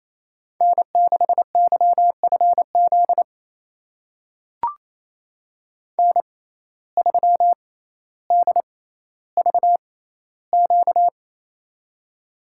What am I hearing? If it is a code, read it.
Morse code, heard as a keyed tone, about 28 wpm, spaced as for 14 wpm, 700 Hz. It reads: N6YFZ N3DVQ
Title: USA Call Signs - 5 Characters - 6x Character Spacing - 28wpm